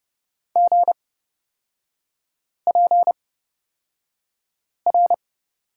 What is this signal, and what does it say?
Morse code, heard as a keyed tone, about 30 wpm, spaced as for 5 wpm, 700 Hz. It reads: GPR
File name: Lesson 38e - Letter q - Three Characters 30wpm